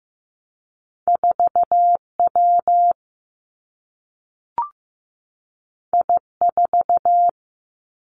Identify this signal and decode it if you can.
Morse code, heard as a keyed tone, about 15 wpm, 700 Hz. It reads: 4 W I 4
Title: Two Challenging Letter-Number Combinations 15wpm